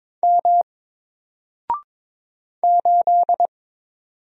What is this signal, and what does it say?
Morse code, heard as a keyed tone, about 22 wpm, 700 Hz. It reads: M 8
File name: Single-Letter-Number-22wpm